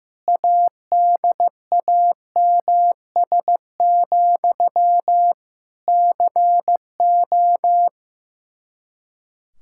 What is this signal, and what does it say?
Morse code, heard as a keyed tone, about 15 wpm, 700 Hz. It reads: ADAMS, CO
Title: Practice QSO - Exchange 1 - Beginning - 15wpm